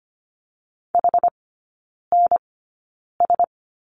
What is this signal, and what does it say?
Morse code, heard as a keyed tone, about 25 wpm, spaced as for 9 wpm, 690 Hz. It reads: HNS
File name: Lesson 15e - Number 5 - Three Characters 25wpm